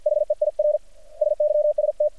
Morse code, heard as a keyed tone, about 50 wpm, 600 Hz. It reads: HEIR SYST
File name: archived / CRSnet